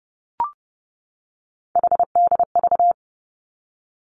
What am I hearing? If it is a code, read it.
Morse code, heard as a keyed tone, about 30 wpm, 700 Hz. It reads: HDV
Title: Three Challenging Letter-Number Combinations 30wpm